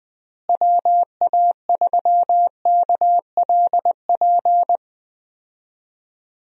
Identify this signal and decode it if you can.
Morse code, heard as a keyed tone, about 20 wpm, 700 Hz. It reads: WA3KLP
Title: USA Call Signs - 6 Characters - Repeated 2x - 20wpm